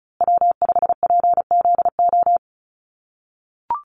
Morse code, heard as a keyed tone, about 34 wpm, 700 Hz. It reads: W5PZO
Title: USA Call Signs Repeated 2x 35wpm